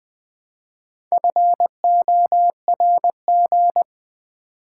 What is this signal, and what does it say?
Morse code, heard as a keyed tone, about 20 wpm, 690 Hz. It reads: FORG
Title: Build A Word - 2 Letters At A Time - 6-Letter Words - 20wpm